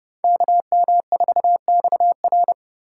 Morse code, heard as a keyed tone, about 37 wpm, 700 Hz. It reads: KM4XR
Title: Speed Racing - USA 2x2 Call Signs 30wpm